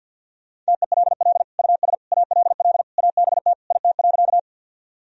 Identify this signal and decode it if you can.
Morse code, heard as a keyed tone, about 49 wpm, 700 Hz. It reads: TELL US ALL ABT IT.
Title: Sentences from Top 200 Words and Abbreviations 50wpm